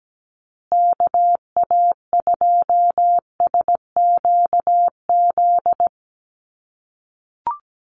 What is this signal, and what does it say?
Morse code, heard as a keyed tone, about 17 wpm, 700 Hz. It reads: KA2SQZ
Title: USA Call Signs Repeated 2x 17wpm